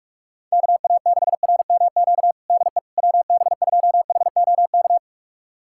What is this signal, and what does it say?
Morse code, heard as a keyed tone, about 46 wpm, 700 Hz. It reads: KA6RMQ DE WB2HOK
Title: USA Callsign DE Callsign - 6 Characters - 45wpm